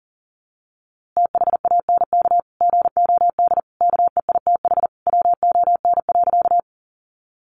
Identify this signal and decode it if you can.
Morse code, heard as a keyed tone, about 40 wpm, 700 Hz. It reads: THANK GOD KEITH WON.
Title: A Sound Of Thunder 40wpm